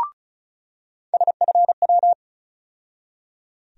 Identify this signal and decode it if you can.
Morse code, heard as a keyed tone, about 36 wpm, 700 Hz. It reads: SFW
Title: USA Call Sign Postfixes - 3 Letters 35wpm